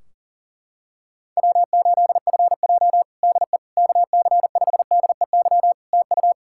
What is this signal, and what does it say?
Morse code, heard as a keyed tone, about 38 wpm, 700 Hz. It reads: W8FJ DE KC5DEY TU
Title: Practice QSO - Exchange 4 - Ending - 40wpm